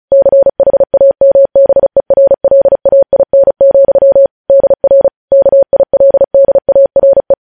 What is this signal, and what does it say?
Morse code, heard as a keyed tone, about 36 wpm, 550 Hz. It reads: CHAMBERLAIN, DR KILDARE
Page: News headlines in Morse code at 35 wpm